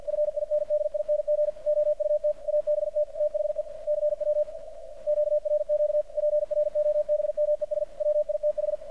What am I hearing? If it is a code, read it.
Morse code, heard as a keyed tone, about 52 wpm, 600 Hz, with radio noise. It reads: FINDING OUT ABT THE CW QRQ PROBMES WITH
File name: archived / CRSnet